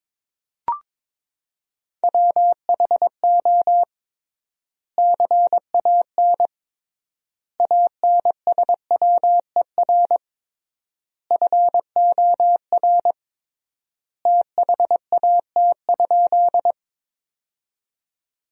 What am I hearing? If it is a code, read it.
Morse code, heard as a keyed tone, about 22 wpm, 700 Hz. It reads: WHO CAN ANSWER FOR THAT?
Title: 3x Word-Spacing - Sentences from Top 300 Words - 22wpm